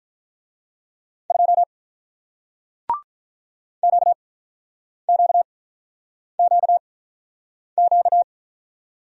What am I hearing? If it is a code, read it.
Morse code, heard as a keyed tone, about 48 wpm, spaced as for 9 wpm, 700 Hz. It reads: 2 QQQQ